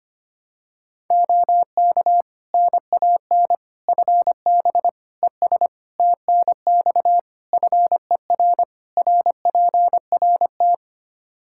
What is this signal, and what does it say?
Morse code, heard as a keyed tone, about 25 wpm, 700 Hz. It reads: OK NAN FB ES TNX FER RPRT
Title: Practice QSO - Exchange 2 - Info - 25wpm